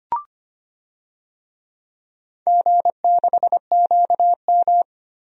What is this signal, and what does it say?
Morse code, heard as a keyed tone, about 25 wpm, 700 Hz. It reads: G6QM
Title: DXCC-100 Call Signs - 4 Characters - Repeated 2x - 25wpm